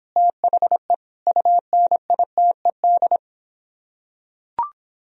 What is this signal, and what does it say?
Morse code, heard as a keyed tone, about 26 wpm, 700 Hz. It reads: THE UNITED